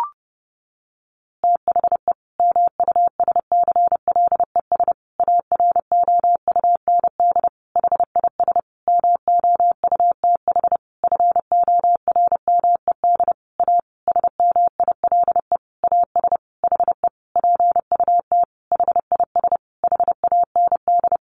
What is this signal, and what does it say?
Morse code, heard as a keyed tone, about 30 wpm, 700 Hz. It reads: THE MUSCLES AROUND HIS MOUTH FORMED A SMILE AS HE PUT HIS HAND
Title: A Sound Of Thunder 30wpm